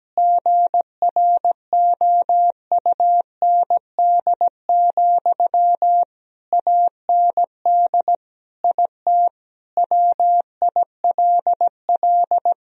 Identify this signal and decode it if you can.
Morse code, heard as a keyed tone, about 17 wpm, 700 Hz. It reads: GROUND, AND IT WILL
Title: Sentences from Top 1000 Words - Review Part 2 - 17wpm